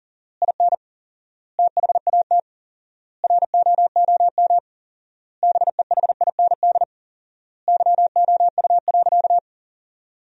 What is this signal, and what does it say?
Morse code, heard as a keyed tone, about 40 wpm, 700 Hz. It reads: IN THAT ROOM BEHIND YOU.
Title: 4x Word-Spacing - Sentences from Top 500 Words - 40wpm